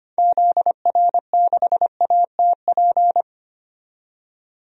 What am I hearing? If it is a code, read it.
Morse code, heard as a keyed tone, about 25 wpm, 700 Hz. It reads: ZR6ATP
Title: DXCC-100 Call Signs - 6 Characters - 25wpm